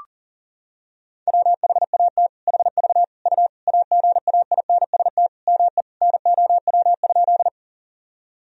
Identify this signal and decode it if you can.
Morse code, heard as a keyed tone, about 40 wpm, 700 Hz. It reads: WHAT HV U AGAINST ME NOW?
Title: Sentences from Top 400 Words and Abbreviations 40wpm